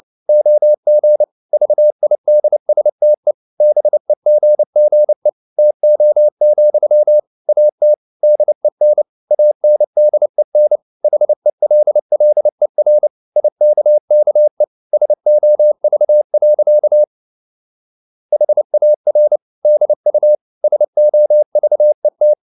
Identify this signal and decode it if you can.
Morse code, heard as a keyed tone, about 29 wpm, 600 Hz. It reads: OG VIDSTE BEGGE TO, AT DEN ANDEN HELLER IKKE SOV. HAR DU SOVET
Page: CW med Gnister